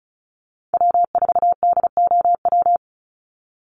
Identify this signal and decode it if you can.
Morse code, heard as a keyed tone, about 34 wpm, 700 Hz. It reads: W4DOW
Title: NA SOTA Chaser Callsigns 35wpm